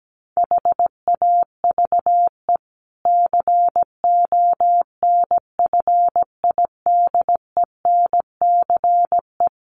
Morse code, heard as a keyed tone, about 17 wpm, 700 Hz. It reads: HAVE CONFIDENCE
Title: Sentences from Top 1500 Words 17wpm